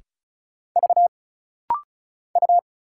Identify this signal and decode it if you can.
Morse code, heard as a keyed tone, about 36 wpm, 700 Hz. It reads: V U
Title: Single Letter-Number Alt - Rapid-Fire 35wpm